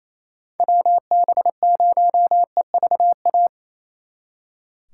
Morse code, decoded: WB0EVA

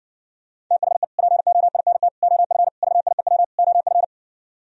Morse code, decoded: THE POINT OF VIEW OF

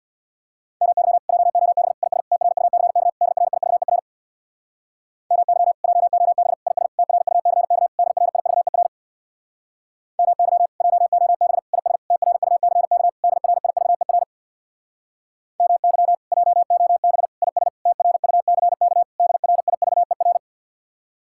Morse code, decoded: MY JOB IS TRUCK DRIVER MY JOB IS TRUCK DRIVER MY JOB IS TRUCK DRIVER MY JOB IS TRUCK DRIVER

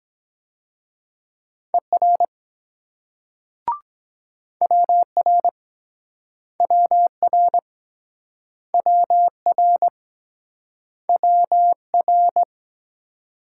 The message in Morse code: ER WR WR WR WR

